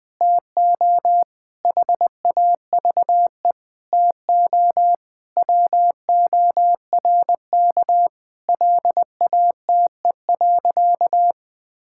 Morse code, heard as a keyed tone, about 21 wpm, 700 Hz. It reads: TO HAVE TO WORK LATE.
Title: Sentences with 5-Word Phrases - Spoken Beginning - 20wpm